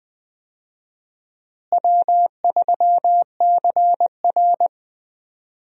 Morse code, heard as a keyed tone, about 20 wpm, 700 Hz. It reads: W3CR
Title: USA Call Signs - 4 Characters - Repeated 2x - 20wpm